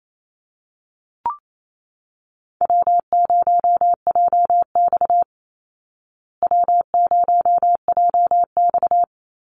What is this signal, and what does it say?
Morse code, heard as a keyed tone, about 28 wpm, 700 Hz. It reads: W0JX W0JX
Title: Call Signs Repeated 2x - 28wpm